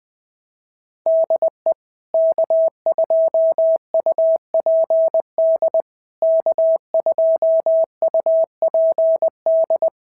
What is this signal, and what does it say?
Morse code, heard as a keyed tone, about 20 wpm, 650 Hz. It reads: DE K2UPD K2UPD
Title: POTA QSO Practice - 20wpm